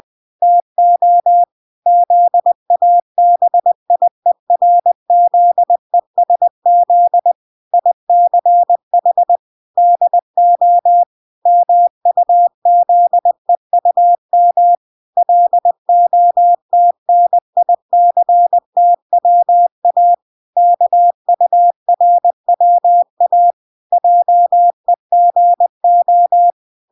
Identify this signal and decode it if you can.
Morse code, heard as a keyed tone, about 20 wpm, 700 Hz. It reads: TO ZABIERZESZ ICH DO MUZEUM LOTNICTWA KURWA JEGO
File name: recordings / 20wpm0000